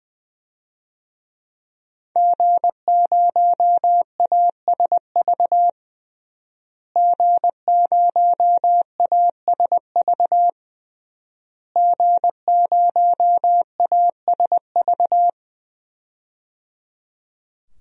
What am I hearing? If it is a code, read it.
Morse code, heard as a keyed tone, about 21 wpm, 700 Hz. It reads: G0ASV G0ASV G0ASV
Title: DXCC-100 Call Signs - 5 Characters - Repeated 3x - 20wpm